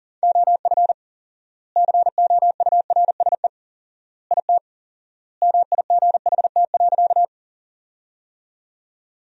Morse code, heard as a keyed tone, about 40 wpm, 700 Hz. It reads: OF COURSE IT MIGHT.
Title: 4x Word-Spacing - Sentences from Top 500 Words - 40wpm